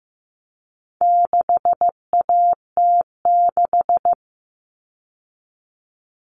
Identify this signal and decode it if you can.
Morse code, heard as a keyed tone, about 15 wpm, 700 Hz. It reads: 6AT6
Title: All Cut Numbers - 4-Digit 15wpm